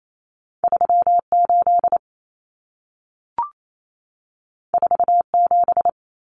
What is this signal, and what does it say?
Morse code, heard as a keyed tone, about 28 wpm, 700 Hz. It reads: 38 47
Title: Numbers - 2-Digit 28wpm